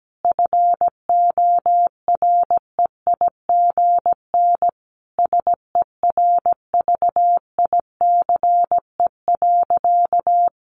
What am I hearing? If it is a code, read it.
Morse code, heard as a keyed tone, about 17 wpm, 700 Hz. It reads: FOREIGN SERVICE.
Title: Sentences from Top 1400 Words 17wpm